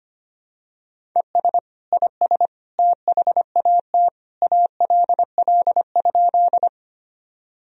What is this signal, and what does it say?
Morse code, heard as a keyed tone, about 25 wpm, 700 Hz. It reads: ES IS THAT ALL?
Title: Sentences from Top 100 Words and Abbreviations 25wpm